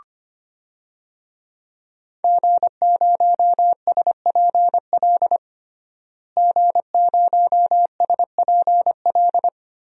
Morse code, heard as a keyed tone, about 25 wpm, 700 Hz. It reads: G0SPL G0SPL
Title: DXCC-100 Call Signs - 5 Characters - Repeated 3x - 25wpm